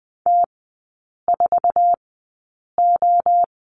Morse code, heard as a keyed tone, about 20 wpm, spaced as for 8 wpm, 700 Hz. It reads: T4O
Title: 047 Lesson 9e - Number 4 - Three Characters 20wpm